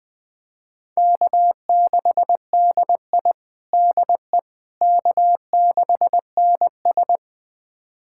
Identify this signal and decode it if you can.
Morse code, heard as a keyed tone, about 20 wpm, 700 Hz. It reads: K6DI DE K6NS
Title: USA Callsign DE Callsign - 4 Characters - 20wpm